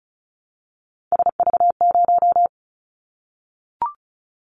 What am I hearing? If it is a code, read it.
Morse code, heard as a keyed tone, about 36 wpm, 700 Hz. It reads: SV0
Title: Callsign Prefixes for Top 50 DXCC - 35wpm